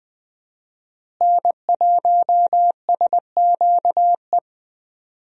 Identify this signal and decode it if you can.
Morse code, heard as a keyed tone, about 21 wpm, 700 Hz. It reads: N1SQE
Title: USA Call Signs - 5 Characters - Repeated 2x - 20wpm